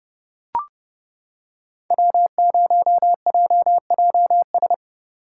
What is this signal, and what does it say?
Morse code, heard as a keyed tone, about 25 wpm, 700 Hz. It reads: W0JJS
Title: Speed Racing - USA 1x3 Call Signs 20wpm